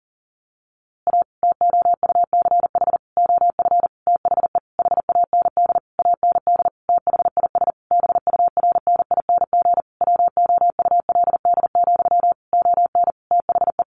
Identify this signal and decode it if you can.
Morse code, heard as a keyed tone, about 40 wpm, 700 Hz. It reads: A TOUCH OF THE HAND AND THIS BURNING WOULD, ON THE
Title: A Sound Of Thunder 40wpm